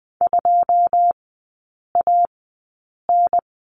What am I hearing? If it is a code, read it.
Morse code, heard as a keyed tone, about 20 wpm, spaced as for 8 wpm, 700 Hz. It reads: 2AN
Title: 081 Lesson 14e - Number 2 - Three Characters 20wpm